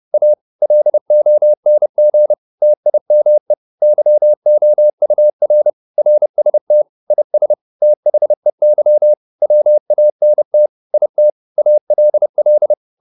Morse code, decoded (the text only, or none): A LONG TIME YOUR RST IS THEY WANT IT ALL